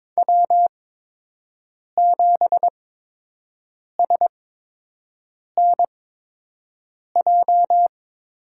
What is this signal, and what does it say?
Morse code, heard as a keyed tone, about 22 wpm, spaced as for 6 wpm, 700 Hz. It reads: W7SNJ
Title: USA Call Signs - 5 Characters - 8x Character Spacing - 22wpm